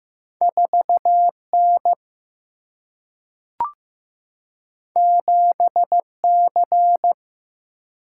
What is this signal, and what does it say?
Morse code, heard as a keyed tone, about 15 wpm, 700 Hz. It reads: 4N 7C
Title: Single-Letter Single-Number 15wpm